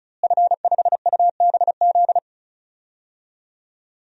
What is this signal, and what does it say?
Morse code, heard as a keyed tone, about 34 wpm, 700 Hz. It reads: F5UBZ
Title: DXCC-100 Call Signs - 5 Characters - Repeated 3x - 35wpm